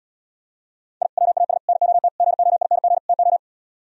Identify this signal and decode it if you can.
Morse code, heard as a keyed tone, about 77 wpm, 700 Hz. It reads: I WAS NOT GOING TO